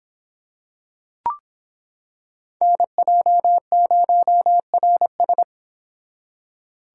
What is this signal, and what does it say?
Morse code, heard as a keyed tone, about 25 wpm, 700 Hz. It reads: NJ0RS